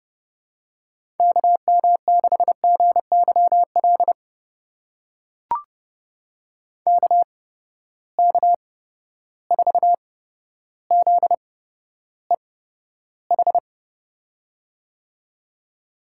Morse code, heard as a keyed tone, about 30 wpm, spaced as for 13 wpm, 700 Hz. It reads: KM6GYL KK4ZEH
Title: USA Call Signs - 6 Characters - 8x Character Spacing - 30wpm